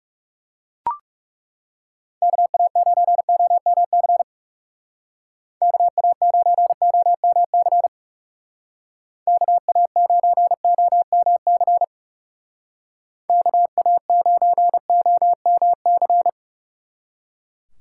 Morse code, heard as a keyed tone, about 39 wpm, 700 Hz. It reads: KA9OMC KA9OMC KA9OMC KA9OMC